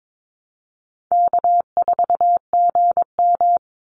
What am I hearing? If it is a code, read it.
Morse code, heard as a keyed tone, about 22 wpm, 700 Hz. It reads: K4GM
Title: Call Signs - 22wpm